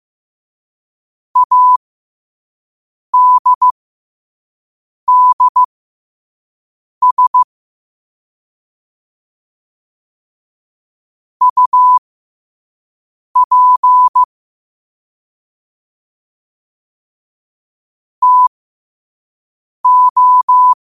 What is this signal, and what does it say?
Morse code, heard as a keyed tone, about 15 wpm, spaced as for 5 wpm, 1000 Hz. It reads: ADDS UP TO
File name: qotdfiles / 5wpm